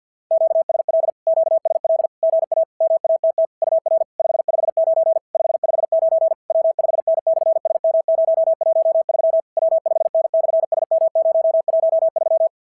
Morse code, heard as a keyed tone, about 46 wpm, 660 Hz. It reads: QSL QSL GA MATT UR 559 559 W5N/SM013 W5N/SM013
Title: SOTA QSO Practice 50wpm